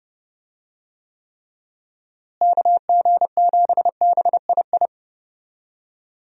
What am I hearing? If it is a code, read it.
Morse code, heard as a keyed tone, about 32 wpm, 700 Hz. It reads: KG7BII
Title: USA Call Signs - 6 Characters - Repeated 2x - 30wpm